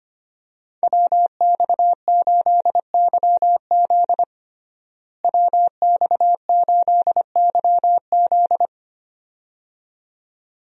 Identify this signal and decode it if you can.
Morse code, heard as a keyed tone, about 25 wpm, 700 Hz. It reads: WX8YZ WX8YZ
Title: USA Call Signs - 5 Characters - Repeated 2x - 25wpm